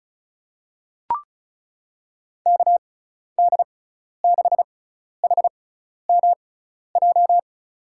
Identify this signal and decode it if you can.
Morse code, heard as a keyed tone, about 34 wpm, spaced as for 12 wpm, 700 Hz. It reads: KD6HMJ
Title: USA Call Signs - 6 Characters - 6x Character Spacing - 35wpm